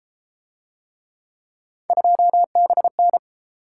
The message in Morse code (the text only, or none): 2BN